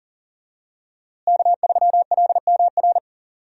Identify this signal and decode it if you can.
Morse code, heard as a keyed tone, about 40 wpm, 700 Hz. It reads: K3LMR
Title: USA Call Signs - 5 Characters - Repeated 2x - 40wpm